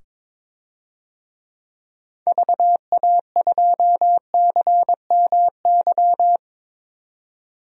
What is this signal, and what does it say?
Morse code, heard as a keyed tone, about 21 wpm, 700 Hz. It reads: VA2CMY
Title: DXCC-100 Call Signs - 6 Characters - 22wpm